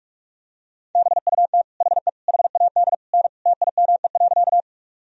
Morse code, heard as a keyed tone, about 46 wpm, 700 Hz. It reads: BUT HE HAD N TIME.
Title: Sentences from Top 100 Words and Abbreviations 45wpm